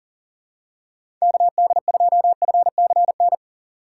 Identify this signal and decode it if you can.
Morse code, heard as a keyed tone, about 40 wpm, 700 Hz. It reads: KD2FCN
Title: USA Call Signs - 6 Characters - 2x Character Spacing - 40wpm